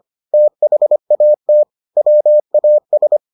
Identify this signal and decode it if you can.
Morse code, heard as a keyed tone, about 25 wpm, 600 Hz. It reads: THAT WAS